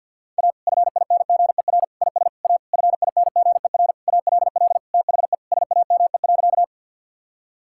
Morse code, heard as a keyed tone, about 50 wpm, 700 Hz. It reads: A FINGER IS A FINGER ALL THE SAME.
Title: Sentences from Top 1600 Words 50wpm